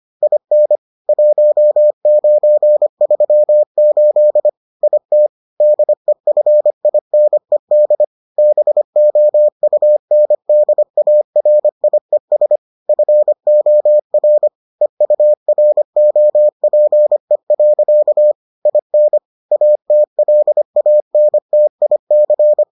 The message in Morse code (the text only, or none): IN 1938 IT DEFINED BOUNDARIES FOR EUROPE. IN ATLANTIC